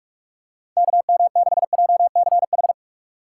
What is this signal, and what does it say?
Morse code, heard as a keyed tone, about 44 wpm, 700 Hz. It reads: KM6JCH
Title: USA Call Signs - 6 Characters - 6x Character Spacing - 45wpm